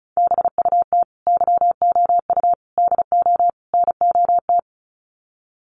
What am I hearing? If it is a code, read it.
Morse code, heard as a keyed tone, about 34 wpm, 700 Hz. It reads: BUT YOU DO NOT
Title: Sets of 4 Words 35wpm